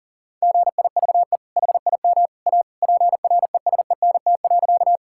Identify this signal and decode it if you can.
Morse code, heard as a keyed tone, about 40 wpm, 700 Hz. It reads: GIVE HIM A PRESENT.
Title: Sentences from Top 1000 Words - Review Part 5 - 40wpm